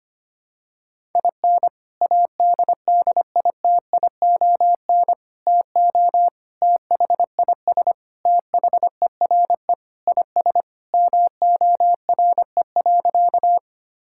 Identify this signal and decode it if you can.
Morse code, heard as a keyed tone, about 25 wpm, 700 Hz. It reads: IN ADDITION TO THIS THERE IS MORE.
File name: practice / 25wpm / Sentences with 3-Word Phrases - 25wpm